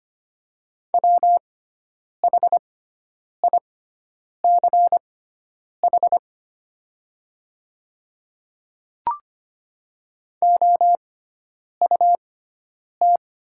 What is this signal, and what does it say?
Morse code, heard as a keyed tone, about 25 wpm, spaced as for 9 wpm, 700 Hz. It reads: WHICH OUT
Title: 6x Character Spacing - Top 300 Words - 25wpm